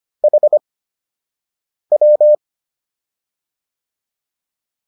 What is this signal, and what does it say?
Morse code, heard as a keyed tone, about 25 wpm, spaced as for 6 wpm, 600 Hz. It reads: HW